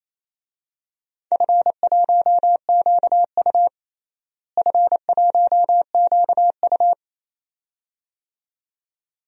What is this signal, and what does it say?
Morse code, heard as a keyed tone, about 27 wpm, 700 Hz. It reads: F1QU F1QU
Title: DXCC-100 Call Signs - 4 Characters - Repeated 2x - 28wpm